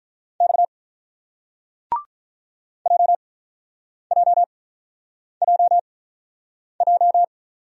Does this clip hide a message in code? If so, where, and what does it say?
Yes: Morse code from 0.4 to 7.2 s, heard as a keyed tone, about 48 wpm, spaced as for 9 wpm, 700 Hz. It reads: X JJJJ